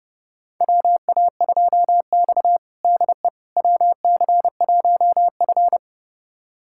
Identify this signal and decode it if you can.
Morse code, heard as a keyed tone, about 30 wpm, 700 Hz. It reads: WA2X DE WC1F
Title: USA Callsign DE Callsign - 4 Characters - 30wpm